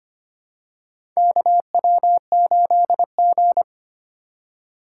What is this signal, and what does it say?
Morse code, heard as a keyed tone, about 25 wpm, 700 Hz. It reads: KW8G
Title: USA Call Signs - 4 Characters - 2x Character Spacing - 25wpm